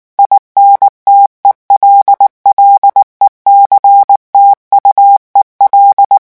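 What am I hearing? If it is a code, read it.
Morse code, heard as a keyed tone, about 19 wpm, 800 Hz. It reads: INTELLECTUEL
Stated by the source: French Morse revision 8 drill of random five-letter groups at 18 words per minute.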